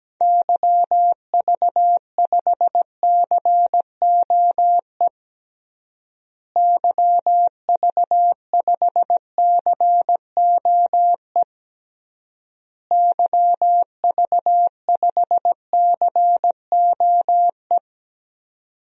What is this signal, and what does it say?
Morse code, heard as a keyed tone, about 17 wpm, 700 Hz. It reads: YV5COE YV5COE YV5COE
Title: DXCC-100 Call Signs - 6 Characters - Repeated 3x - 17wpm